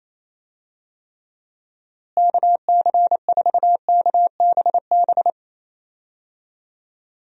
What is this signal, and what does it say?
Morse code, heard as a keyed tone, about 28 wpm, 700 Hz. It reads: KC4KBB
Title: USA Call Signs - 6 Characters - Repeated 3x - 28wpm